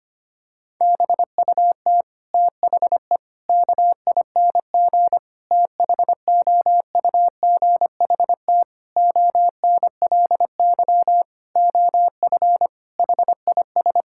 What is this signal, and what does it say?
Morse code, heard as a keyed tone, about 25 wpm, 700 Hz. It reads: BUT THE KING THOUGHT ONLY OF HIS
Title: Sentences from Top 1600 Words 25wpm